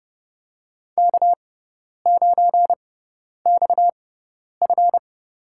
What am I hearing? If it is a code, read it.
Morse code, heard as a keyed tone, about 30 wpm, spaced as for 10 wpm, 700 Hz. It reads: K9XF
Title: USA Call Signs - 6x Character Spacing - 30wpm